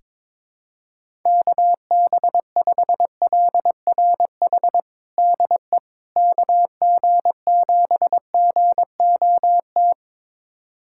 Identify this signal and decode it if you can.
Morse code, heard as a keyed tone, about 21 wpm, 700 Hz. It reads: KB5LRH DE KG7GOT
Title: USA Callsign DE Callsign 22wpm